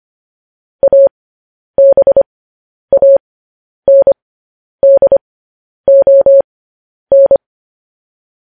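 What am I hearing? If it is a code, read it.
Morse code, heard as a keyed tone, about 25 wpm, spaced as for 10 wpm, 550 Hz. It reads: ABANDON
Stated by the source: Morse code practice of news headlines at 10 wpm.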